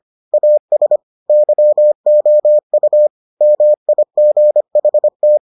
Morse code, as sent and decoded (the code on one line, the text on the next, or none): .- ... / -.-- --- ..- / -- .. --. .... -
AS YOU MIGHT